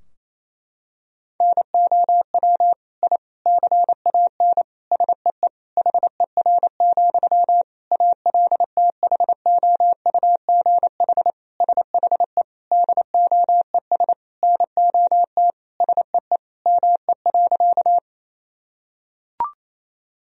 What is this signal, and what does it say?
Morse code, heard as a keyed tone, about 28 wpm, 700 Hz. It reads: NOW I CAN SEE HER, ALTHOUGH SHE DOES NOT SEE ME.
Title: Sentences from Top 1000 Words - Review Part 3 - 28wpm